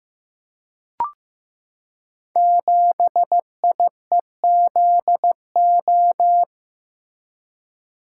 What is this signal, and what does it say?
Morse code, heard as a keyed tone, about 15 wpm, 700 Hz. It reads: 7IEZO
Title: Code Groups - 5 Letter:Numbers - 15wpm